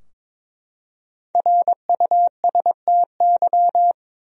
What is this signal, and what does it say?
Morse code, heard as a keyed tone, about 22 wpm, 700 Hz. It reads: RUSTY
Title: USA Common Names - Max 5 Letters - 22wpm